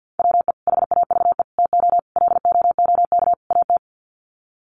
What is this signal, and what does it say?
Morse code, heard as a keyed tone, about 50 wpm, 700 Hz. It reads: WE HAVE TO LOOK AT
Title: Sets of 5 Words Encore2 50wpm